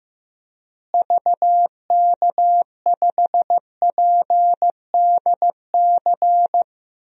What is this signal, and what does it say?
Morse code, heard as a keyed tone, about 15 wpm, 700 Hz. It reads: VK5PDC
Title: DXCC-100 Call Signs - 6 Characters - Repeated 2x - 15wpm